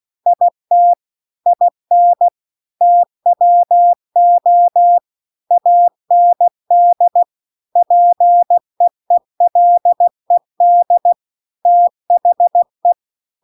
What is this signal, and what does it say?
Morse code, heard as a keyed tone, about 16 wpm, 700 Hz. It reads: IT IN TWO AND PEELED THE
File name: CW-The-Big-Twoheaded-River-2-16WPM